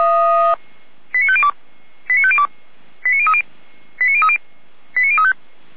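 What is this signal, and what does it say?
Five-tone selective call (ZVEI), heard as quick runs of five tones, tones of about 70 ms. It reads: 89502 89502 89030 89030 89035
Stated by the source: ZVEI five-tone call 89042 received by SDR.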